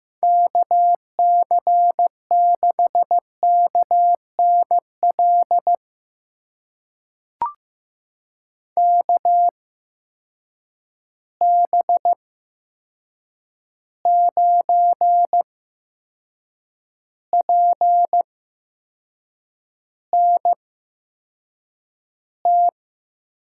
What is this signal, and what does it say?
Morse code, heard as a keyed tone, about 15 wpm, spaced as for 6 wpm, 700 Hz. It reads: KC6KNL KB9PNT
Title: USA Call Signs - 8x Character Spacing - 15wpm